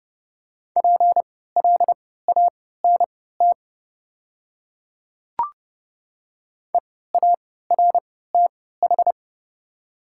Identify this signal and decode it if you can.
Morse code, heard as a keyed tone, about 30 wpm, spaced as for 17 wpm, 700 Hz. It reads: PLANT EARTH
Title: 3x Character Spacing - Top 300 Words - 30wpm